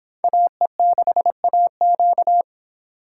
Morse code, heard as a keyed tone, about 24 wpm, 700 Hz. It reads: AE6AQ